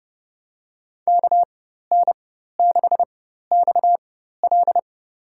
Morse code, heard as a keyed tone, about 32 wpm, spaced as for 14 wpm, 700 Hz. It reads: KN6XL
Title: USA Call Signs - 4x Character Spacing - 30wpm